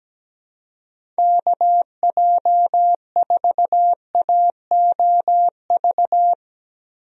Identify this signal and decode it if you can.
Morse code, heard as a keyed tone, about 17 wpm, 700 Hz. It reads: KJ4AOV
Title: USA Call Signs - 6 Characters - Repeated 2x - 17wpm